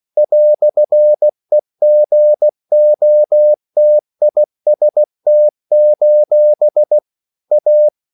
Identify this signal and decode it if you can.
Morse code, heard as a keyed tone, about 16 wpm, 600 Hz. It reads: "EGOTIST: A
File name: CW / Practice / EnglishQuotations0216